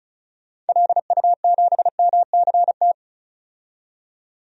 Morse code, heard as a keyed tone, about 34 wpm, 700 Hz. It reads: LU7MCT
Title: DXCC-100 Call Signs - 6 Characters - 35wpm